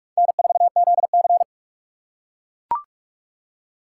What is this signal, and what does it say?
Morse code, heard as a keyed tone, about 45 wpm, 700 Hz. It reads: N4ZC
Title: USA Call Signs - 4 Characters - 2x Character Spacing - 45wpm